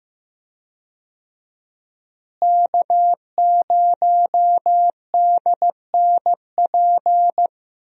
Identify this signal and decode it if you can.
Morse code, heard as a keyed tone, about 15 wpm, 700 Hz. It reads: K0DNP
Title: USA Call Signs - 5 Characters - Repeated 2x - 15wpm